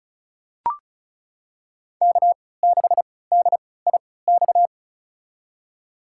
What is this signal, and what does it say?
Morse code, heard as a keyed tone, about 36 wpm, spaced as for 20 wpm, 700 Hz. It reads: K6DIX
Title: USA Call Signs - 5 Characters - 3x Character Spacing - 35wpm